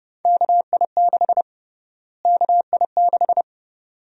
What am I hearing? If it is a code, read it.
Morse code, heard as a keyed tone, about 30 wpm, 700 Hz. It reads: KI6 KI6
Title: USA Call Sign Prefixes - Repeated 2x - 30wpm